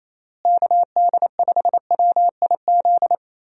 Morse code, heard as a keyed tone, about 28 wpm, 700 Hz. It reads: KD5WIZ
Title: USA Call Signs - 6 Characters - Repeated 2x - 28wpm